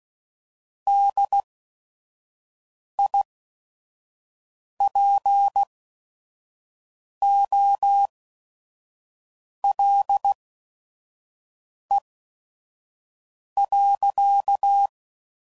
Morse code, heard as a keyed tone, about 16 wpm, spaced as for 5 wpm, 800 Hz. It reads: DIPOLE.
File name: CW Code Groups @5WPM / QSOs / WD0FBT 2